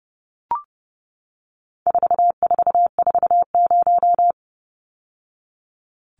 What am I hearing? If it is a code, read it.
Morse code, heard as a keyed tone, about 30 wpm, 700 Hz. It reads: 4440
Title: Numbers - 4-Digit 30wpm